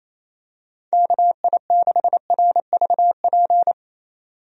Morse code, heard as a keyed tone, about 28 wpm, 700 Hz. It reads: KI6RVP
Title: USA Call Signs - 6 Characters - 28wpm